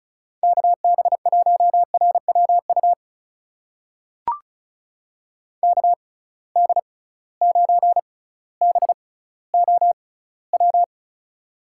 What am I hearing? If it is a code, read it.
Morse code, heard as a keyed tone, about 36 wpm, spaced as for 18 wpm, 700 Hz. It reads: KB1RWU KD9BOW
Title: USA Call Signs - 6 Characters - 6x Character Spacing - 35wpm